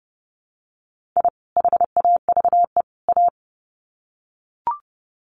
Morse code, heard as a keyed tone, about 30 wpm, 700 Hz. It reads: I HAVE A
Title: Sets of 3 Words 30wpm